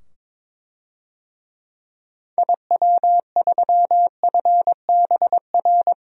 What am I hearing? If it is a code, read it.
Morse code, heard as a keyed tone, about 22 wpm, 700 Hz. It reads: IW3FBR
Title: DXCC-100 Call Signs - 6 Characters - 22wpm